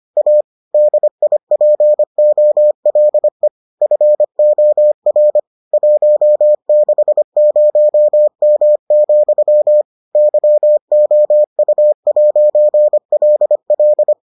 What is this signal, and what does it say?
Morse code, heard as a keyed tone, about 25 wpm, 600 Hz. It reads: A DIPOLE FOR 160M, YOU'LL